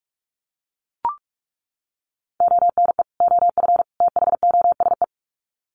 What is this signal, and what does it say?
Morse code, heard as a keyed tone, about 44 wpm, 700 Hz. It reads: ONE OF THOSE
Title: Sets of 3 Words 45wpm